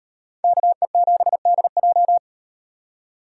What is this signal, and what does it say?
Morse code, heard as a keyed tone, about 35 wpm, 700 Hz. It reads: KE7DJ